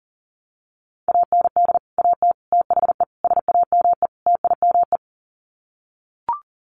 Morse code, heard as a keyed tone, about 40 wpm, 700 Hz. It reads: AND AT THE SAME TIME
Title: Sets of 5 Words 40wpm